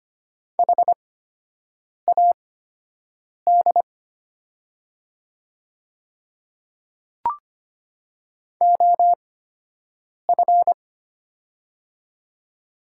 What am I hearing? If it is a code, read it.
Morse code, heard as a keyed tone, about 25 wpm, spaced as for 7 wpm, 700 Hz. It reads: HAD OF
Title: 8x Character Spacing - Top 100 Words - 25wpm